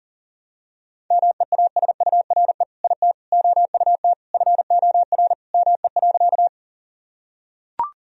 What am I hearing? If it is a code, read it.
Morse code, heard as a keyed tone, about 40 wpm, 700 Hz. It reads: MEASURE IT OUT FOR ME.
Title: Sentences from Top 1000 Words - Review Part 2 - 40wpm